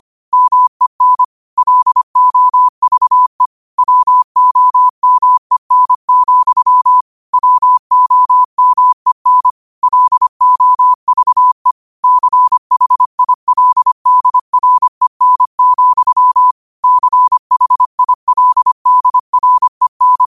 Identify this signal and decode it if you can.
Morse code, heard as a keyed tone, about 25 wpm, 1000 Hz. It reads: MEN LOVE WOMEN, WOMEN LOVE CHILDREN, CHILDREN